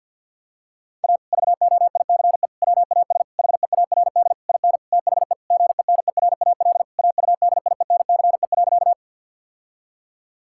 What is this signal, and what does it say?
Morse code, heard as a keyed tone, about 50 wpm, 700 Hz. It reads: A VOICE WAS HEARD IN THE GENERAL AUDIENCE.
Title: Sentences from Top 1000 Words - Review Part 5 - 50wpm